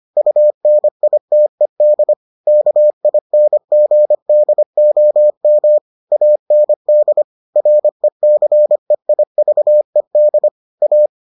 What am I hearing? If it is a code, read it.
Morse code, heard as a keyed tone, about 25 wpm, 600 Hz. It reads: UNITED KINGDOM AND RECEIVED A